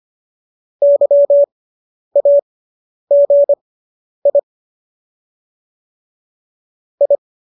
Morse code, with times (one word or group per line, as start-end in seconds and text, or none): 0.8-4.4 YAGI
7.0-7.2 I